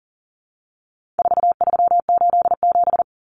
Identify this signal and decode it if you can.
Morse code, heard as a keyed tone, about 40 wpm, 700 Hz. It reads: 4387
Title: Numbers - 4-Digit 40wpm